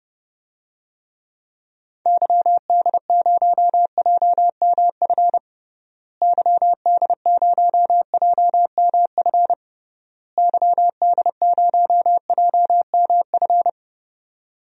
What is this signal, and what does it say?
Morse code, heard as a keyed tone, about 30 wpm, 700 Hz. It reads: YD0JMF YD0JMF YD0JMF
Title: DXCC-100 Call Signs - 6 Characters - Repeated 3x - 30wpm